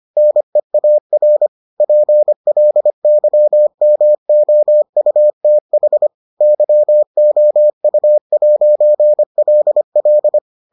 Morse code, heard as a keyed tone, about 25 wpm, 600 Hz. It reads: NEAR PLYMOUTH YOU'LL